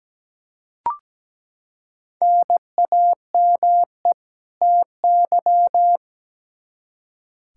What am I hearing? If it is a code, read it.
Morse code, heard as a keyed tone, about 17 wpm, 700 Hz. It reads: NAME TY
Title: Practice QSO - Exchange 1 - Beginning - 17wpm